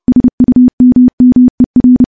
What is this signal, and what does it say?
Morse code, heard as a keyed tone, about 28 wpm, 250 Hz. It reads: SUMMER
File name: morse / morse-training-30WPM